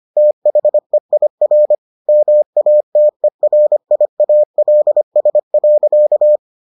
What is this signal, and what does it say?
Morse code, heard as a keyed tone, about 25 wpm, 600 Hz. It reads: THEIR MATERIALS.